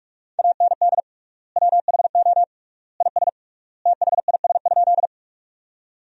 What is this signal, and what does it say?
Morse code, heard as a keyed tone, about 46 wpm, 700 Hz. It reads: AND WHO IS THIS?
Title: 3x Word-Spacing - Sentences from Top 100 Words - 45wpm